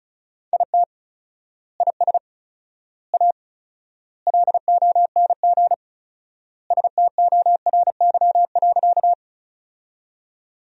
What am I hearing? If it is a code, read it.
Morse code, heard as a keyed tone, about 36 wpm, 700 Hz. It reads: IT IS A LONG STORY.
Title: 4x Word-Spacing - Sentences from Top 300 Words - 35wpm